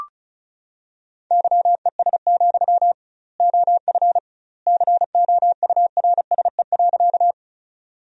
Morse code, heard as a keyed tone, about 34 wpm, 700 Hz. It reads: YES, OF COURSE.
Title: 2x Word-Spacing - Sentences from Top 500 Words - 35wpm